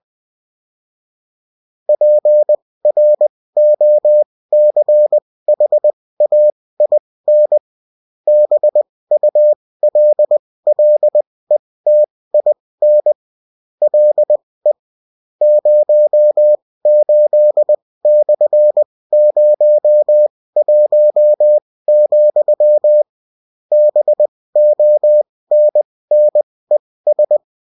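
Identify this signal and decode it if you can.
Morse code, heard as a keyed tone, about 20 wpm, spaced as for 16 wpm, 600 Hz. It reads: PROCHAIN BULLETIN LE 08/01, BONNES
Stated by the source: Bulletin in text and CW form.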